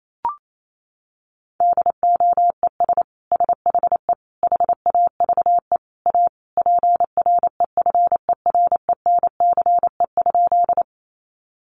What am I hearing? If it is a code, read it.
Morse code, heard as a keyed tone, about 28 wpm, 700 Hz. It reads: DOES SHE HAVE A PREFERENCE?
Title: Sentences from Top 1500 Words 28wpm